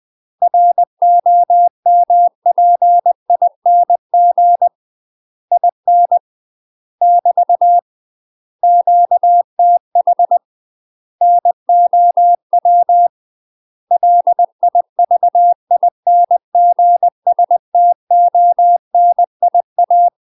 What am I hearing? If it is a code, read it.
Morse code, heard as a keyed tone, about 20 wpm, 700 Hz. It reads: ROMPING IN <BT> QTH NOW LIVINGSTONIA
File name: UK MASTER QSOs TRAINING / 20WPM / UK MASTER QSOs 0035